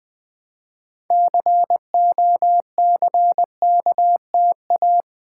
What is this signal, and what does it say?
Morse code, heard as a keyed tone, about 20 wpm, 700 Hz. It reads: COCKTA
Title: Build A Word - 3 Letters At A Time - 8-Letter Words - 20wpm